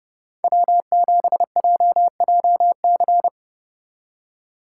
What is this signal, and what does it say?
Morse code, heard as a keyed tone, about 39 wpm, 700 Hz. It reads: W7JJC